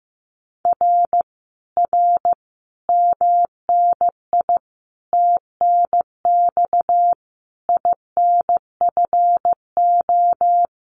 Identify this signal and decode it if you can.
Morse code, heard as a keyed tone, about 15 wpm, 700 Hz. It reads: R R MNI TNX INFO
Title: Common QSO Phrases Encore 15wpm